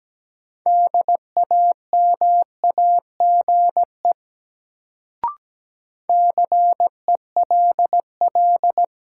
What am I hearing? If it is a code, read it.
Morse code, heard as a keyed tone, about 17 wpm, 700 Hz. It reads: DAMAGE CELL
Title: Sentences from Top 1100 Words - New Word Drill - 17wpm